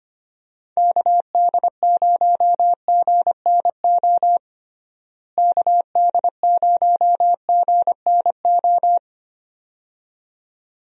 Morse code, heard as a keyed tone, about 25 wpm, 700 Hz. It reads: KD0GNO KD0GNO
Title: USA Call Signs - 6 Characters - Repeated 2x - 25wpm